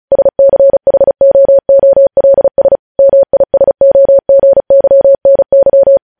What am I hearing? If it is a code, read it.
Morse code, heard as a keyed tone, about 36 wpm, 550 Hz. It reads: SCHOOLS MISOGYNY
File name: podcast / News headlines-1x-35x35